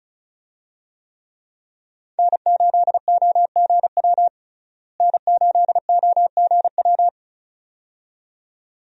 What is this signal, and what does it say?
Morse code, heard as a keyed tone, about 36 wpm, 700 Hz. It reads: N8OGW N8OGW
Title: USA Call Signs - 5 Characters - Repeated 2x - 35wpm